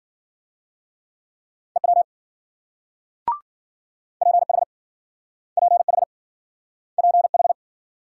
Morse code, heard as a keyed tone, about 53 wpm, 700 Hz. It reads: EW PH PH PH